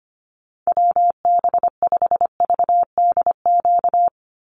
Morse code, heard as a keyed tone, about 25 wpm, 700 Hz. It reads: WB5VDQ
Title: USA Call Signs 25wpm